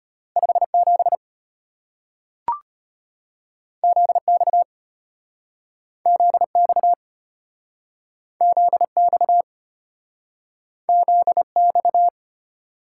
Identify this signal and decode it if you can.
Morse code, heard as a keyed tone, about 34 wpm, 700 Hz. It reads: 57 ZX ZX ZX ZX